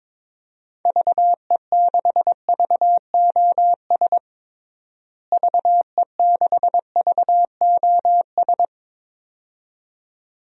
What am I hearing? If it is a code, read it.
Morse code, heard as a keyed tone, about 21 wpm, 700 Hz. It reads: VE6VOS VE6VOS
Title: DXCC-100 Call Signs - 6 Characters - Repeated 2x - 22wpm